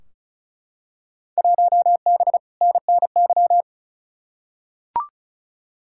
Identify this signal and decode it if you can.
Morse code, heard as a keyed tone, about 36 wpm, 700 Hz. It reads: 1B NNY
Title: Field Day Exchanges - 35wpm